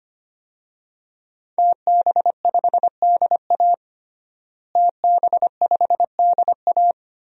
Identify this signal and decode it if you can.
Morse code, heard as a keyed tone, about 25 wpm, 700 Hz. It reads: TB5DA TB5DA
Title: DXCC-100 Call Signs - 5 Characters - Repeated 2x - 25wpm